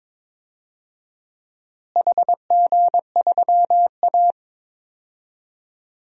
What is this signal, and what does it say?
Morse code, heard as a keyed tone, about 22 wpm, 700 Hz. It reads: HG3A
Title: DXCC-100 Call Signs - 4 Characters - Repeated 2x - 22wpm